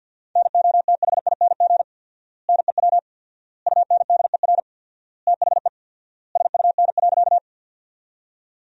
Morse code, heard as a keyed tone, about 51 wpm, 700 Hz. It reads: NOTHING NEW UNDER THE SUN.
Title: 4x Word-Spacing - Sentences from Top 500 Words - 50wpm